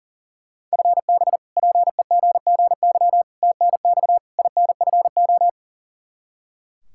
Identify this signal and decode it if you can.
Morse code, heard as a keyed tone, about 40 wpm, 700 Hz. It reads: FB PEGGY TNX INFO
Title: Practice QSO - Exchange 3 - More Info - 40wpm